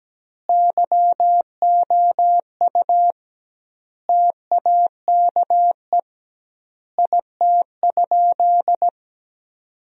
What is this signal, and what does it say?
Morse code, heard as a keyed tone, about 17 wpm, 700 Hz. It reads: YOU TAKE IT?
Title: 2x Word-Spacing - Sentences from Top 200 Words - 17wpm